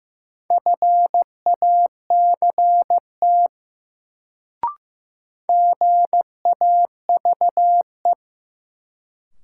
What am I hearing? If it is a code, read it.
Morse code, heard as a keyed tone, about 15 wpm, 700 Hz. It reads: FACT GAVE
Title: Sentences from Top 500 Words - New Word Drill - 15wpm